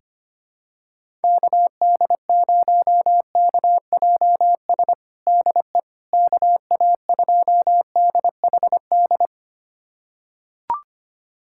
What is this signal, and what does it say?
Morse code, heard as a keyed tone, about 25 wpm, 700 Hz. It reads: KD0KJS DE KA2DHD
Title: USA Callsign DE Callsign - 6 Characters - 25wpm